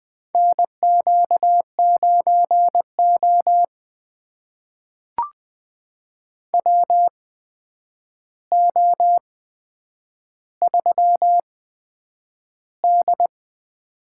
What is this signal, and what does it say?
Morse code, heard as a keyed tone, about 20 wpm, spaced as for 9 wpm, 700 Hz. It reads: NQ9O WO3D
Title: USA Call Signs - 4 Characters - 8x Character Spacing - 20wpm